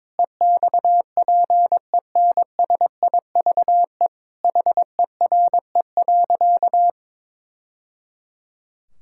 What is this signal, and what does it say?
Morse code, heard as a keyed tone, about 22 wpm, 700 Hz. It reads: EXPENSIVE HERE.
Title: Sentences from Top 1000 Words - Review Part 6 - 22wpm